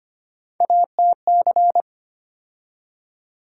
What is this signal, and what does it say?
Morse code, heard as a keyed tone, about 25 wpm, 700 Hz. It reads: ATC
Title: USA Call Sign Postfixes - 3 Letters 25wpm